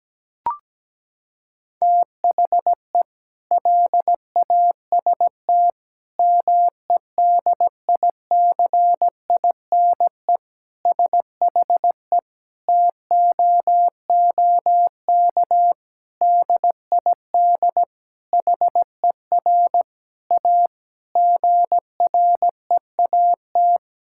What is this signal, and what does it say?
Morse code, heard as a keyed tone, about 17 wpm, 700 Hz. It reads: THE LAST MEDICINE SHE TOOK DID HER A GREAT
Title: Sentences from Top 1000 Words - Review Part 6 - 17wpm